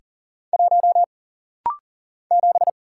Morse code, heard as a keyed tone, about 40 wpm, 700 Hz. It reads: 1 7
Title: Single Letter-Number Alt - Rapid-Fire 40wpm